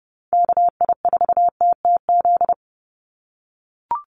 Morse code, heard as a keyed tone, about 30 wpm, 700 Hz. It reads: KI4TTZ
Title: USA Call Signs 30wpm